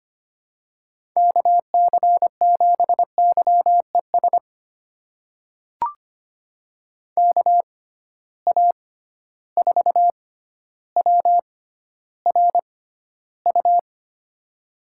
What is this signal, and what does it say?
Morse code, heard as a keyed tone, about 25 wpm, spaced as for 13 wpm, 700 Hz. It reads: KC7YES KA4WRU
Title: USA Call Signs - 6 Characters - 6x Character Spacing - 25wpm